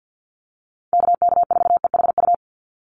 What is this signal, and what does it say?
Morse code, heard as a keyed tone, about 50 wpm, 700 Hz. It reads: KK4EHU
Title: USA Call Signs 50wpm